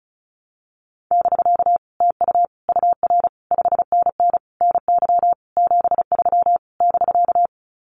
Morse code, heard as a keyed tone, about 36 wpm, 700 Hz. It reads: <BK> TU UR 5NN NY 73 <BK>
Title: Common QSO Phrases Encore 35wpm